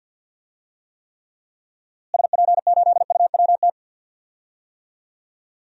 Morse code, heard as a keyed tone, about 46 wpm, 700 Hz. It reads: SP8UWT
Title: DXCC-100 Call Signs - 6 Characters - Repeated 2x - 50wpm